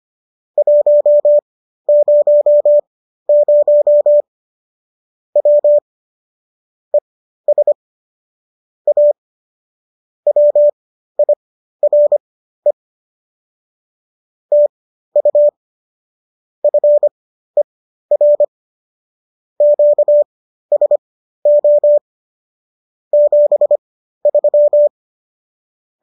Morse code, heard as a keyed tone, about 25 wpm, spaced as for 13 wpm, 590 Hz. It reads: 100 W ES A WIRE TU FER QSO 73